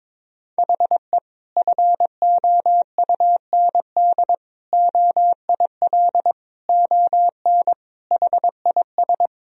Morse code, heard as a keyed tone, about 22 wpm, 700 Hz. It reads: HE FOUND OIL ON HIS
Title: Sentences from Top 1000 Words - Review Part 4 - 22wpm